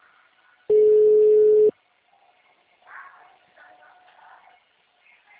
Ringback tone (Europe, 425 Hz).